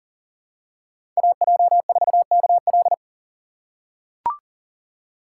Morse code, heard as a keyed tone, about 40 wpm, 700 Hz. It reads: AJ4KL
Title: USA Call Signs - 5 Characters - 6x Character Spacing - 40wpm